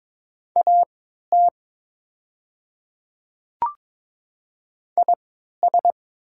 Morse code, heard as a keyed tone, about 22 wpm, spaced as for 12 wpm, 700 Hz. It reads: AT IS
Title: 3x Character Spacing - Top 100 Words - 22wpm